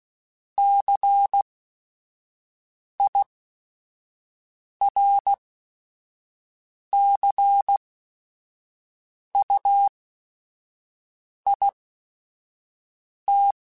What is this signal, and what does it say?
Morse code, heard as a keyed tone, about 16 wpm, spaced as for 5 wpm, 800 Hz. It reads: CIRCUIT
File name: Technical Subjects / Technical Subjects-mp3s / Sw Pwr Sup @5